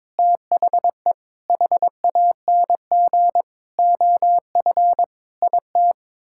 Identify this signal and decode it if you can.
Morse code, heard as a keyed tone, about 22 wpm, 700 Hz. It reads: THE HANG OF IT
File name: practice / 22wpm / Sets of 7 Words - 22wpm